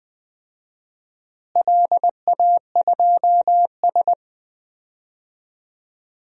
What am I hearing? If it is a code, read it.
Morse code, heard as a keyed tone, about 20 wpm, 700 Hz. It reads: LA2S
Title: DXCC-100 Call Signs - 4 Characters - Repeated 3x - 20wpm